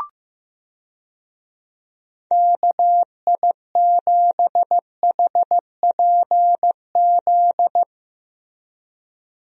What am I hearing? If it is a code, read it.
Morse code, heard as a keyed tone, about 15 wpm, 700 Hz. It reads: KI7HPZ
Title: USA Call Signs - 6 Characters - Repeated 2x - 15wpm